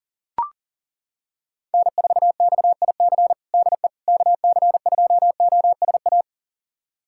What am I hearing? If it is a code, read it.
Morse code, heard as a keyed tone, about 39 wpm, 700 Hz. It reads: N4XIC DE KC2OSA
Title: USA Callsign DE Callsign 40wpm